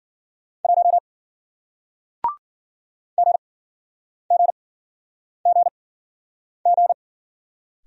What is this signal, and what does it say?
Morse code, heard as a keyed tone, about 51 wpm, spaced as for 9 wpm, 700 Hz. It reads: 1 GGGG